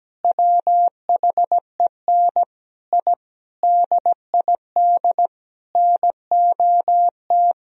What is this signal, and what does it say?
Morse code, heard as a keyed tone, about 17 wpm, 700 Hz. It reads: WHEN I DID NOT
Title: Sentences from Top 1000 Words - Review Part 4 - 17wpm